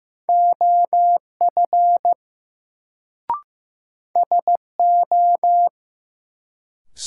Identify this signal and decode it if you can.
Morse code, heard as a keyed tone, about 15 wpm, 700 Hz. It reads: OF SO
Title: Sets of 2 Words - New Word Drill - 15wpm